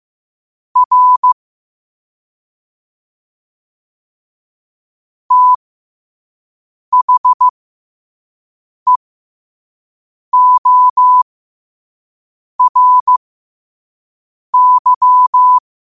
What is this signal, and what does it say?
Morse code, heard as a keyed tone, about 15 wpm, spaced as for 5 wpm, 1000 Hz. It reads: R THEORY
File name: qotdfiles / 5wpm